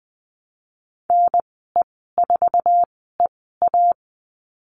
Morse code, heard as a keyed tone, about 20 wpm, spaced as for 14 wpm, 700 Hz. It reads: NE4EA
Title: 048 Lesson 9g - Number 4 - Call Signs 20wpm